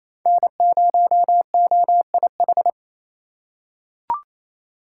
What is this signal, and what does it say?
Morse code, heard as a keyed tone, about 28 wpm, 700 Hz. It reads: N0OIH
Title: USA Call Signs - 5 Characters - 2x Character Spacing - 28wpm